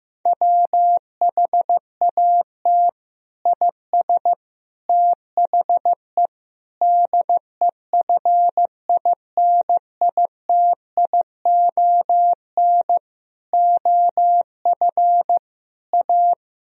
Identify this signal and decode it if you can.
Morse code, heard as a keyed tone, about 15 wpm, 700 Hz. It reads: WHAT IS THE DEFINITION OF A
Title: Sentences from Top 1000 Words - Review Part 4 - 15wpm